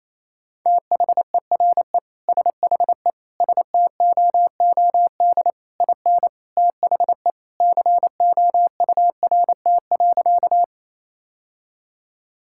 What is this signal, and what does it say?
Morse code, heard as a keyed tone, about 28 wpm, 700 Hz. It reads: THERE SHE STOOD IN THE COURT.
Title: Sentences from Top 1600 Words 28wpm